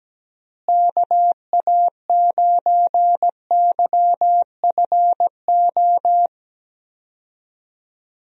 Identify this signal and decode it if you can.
Morse code, heard as a keyed tone, about 17 wpm, 700 Hz. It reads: KA9YFO
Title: USA Call Signs - 6 Characters - Repeated 2x - 17wpm